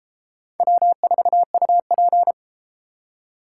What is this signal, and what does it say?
Morse code, heard as a keyed tone, about 32 wpm, 700 Hz. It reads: W4UP